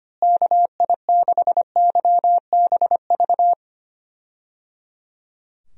Morse code, heard as a keyed tone, about 25 wpm, 700 Hz. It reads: KI6YBV
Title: USA Call Signs - 6 Characters - Repeated 3x - 25wpm